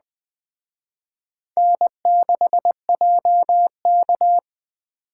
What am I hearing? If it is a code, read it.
Morse code, heard as a keyed tone, about 20 wpm, 700 Hz. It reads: N6JK